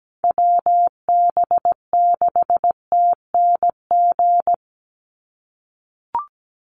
Morse code, heard as a keyed tone, about 17 wpm, 700 Hz. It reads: WB6TNG
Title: USA Call Signs 17wpm